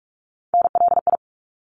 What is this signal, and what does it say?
Morse code, heard as a keyed tone, about 46 wpm, 700 Hz. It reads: NLI
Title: ARRL Sections 45wpm